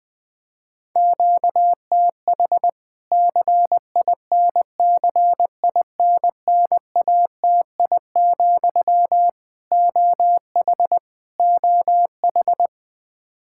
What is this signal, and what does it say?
Morse code, heard as a keyed tone, about 20 wpm, 700 Hz. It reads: QTH CINCINNATI, OH OH
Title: K1USN SST QSO Practice 20wpm